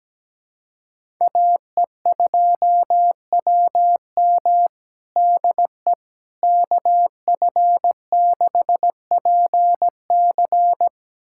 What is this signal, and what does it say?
Morse code, heard as a keyed tone, about 17 wpm, 700 Hz. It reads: AE2WM DE KF6PC
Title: USA Callsign DE Callsign - 5 Characters - 17wpm